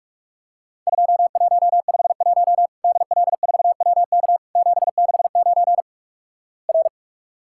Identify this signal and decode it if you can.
Morse code, heard as a keyed tone, about 46 wpm, 700 Hz. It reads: 2151 DL4WK 769 R
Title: Worked All Europe DX Contest - QTC Exchange - 45wpm